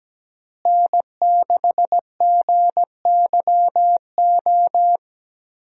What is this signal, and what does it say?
Morse code, heard as a keyed tone, about 17 wpm, 700 Hz. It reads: N6GYO
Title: USA Call Signs - 5 Characters - Repeated 3x - 17wpm